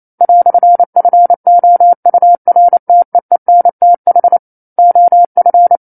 Morse code, heard as a keyed tone, about 28 wpm, 700 Hz. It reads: "FOURTEENTH OF
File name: 09-Chapter-09 28 WPM